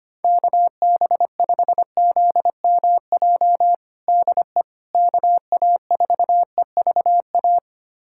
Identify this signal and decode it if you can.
Morse code, heard as a keyed tone, about 25 wpm, 700 Hz. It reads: KB5ZMJ DE KA4EVA
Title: USA Callsign DE Callsign - 6 Characters - 25wpm